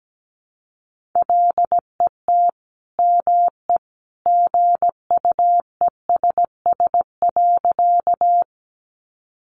Morse code, heard as a keyed tone, about 17 wpm, 700 Hz. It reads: LET ME GUESS.
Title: Sentences from Top 1500 Words 17wpm